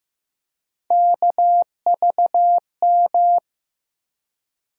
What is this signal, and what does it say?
Morse code, heard as a keyed tone, about 15 wpm, 700 Hz. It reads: KVM
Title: USA Call Sign Postfixes - 3 Letters 15wpm